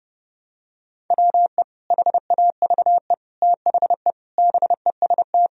Morse code, heard as a keyed tone, about 28 wpm, 700 Hz. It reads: WE HAVE THE BEST
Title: Sentences from Top 1000 Words - Review Part 4 - 30wpm